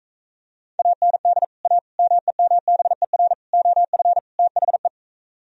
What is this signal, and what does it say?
Morse code, heard as a keyed tone, about 43 wpm, 700 Hz. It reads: AND A MEMBER OF THE